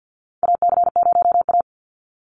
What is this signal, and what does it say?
Morse code, heard as a keyed tone, about 50 wpm, 700 Hz. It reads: AC0A
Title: NA SOTA Chaser Callsigns 50wpm